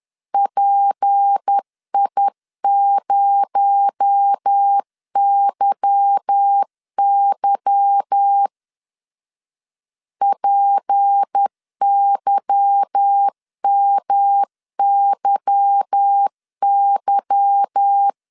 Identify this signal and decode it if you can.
Morse code, heard as a keyed tone, about 10 wpm, 800 Hz. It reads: PI0YY PYMYY
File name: LEZIONI / 8 YJ45 / 8 Y 10wpm 2615